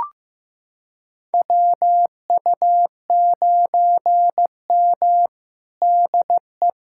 Morse code, heard as a keyed tone, about 16 wpm, 700 Hz. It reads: WU9M DE
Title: USA Callsign DE Callsign - 4 Characters - 15wpm